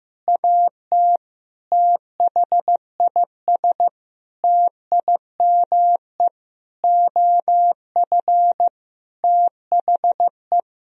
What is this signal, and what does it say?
Morse code, heard as a keyed tone, about 15 wpm, 700 Hz. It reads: AT THIS TIME OF THE
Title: Sentences from Top 1000 Words - Review Part 1 - 15wpm